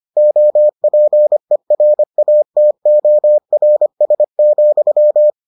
Morse code, heard as a keyed tone, about 25 wpm, 600 Hz. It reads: OPERATORS,